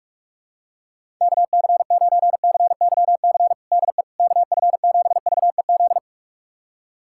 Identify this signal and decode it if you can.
Morse code, heard as a keyed tone, about 46 wpm, 700 Hz. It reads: KC9CYC DE KF7VEZ
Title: USA Callsign DE Callsign 45wpm